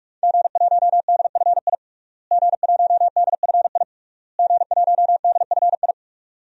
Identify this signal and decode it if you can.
Morse code, heard as a keyed tone, about 43 wpm, 700 Hz. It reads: G1DFI G1DFI G1DFI
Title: DXCC-100 Call Signs - 5 Characters - Repeated 3x - 45wpm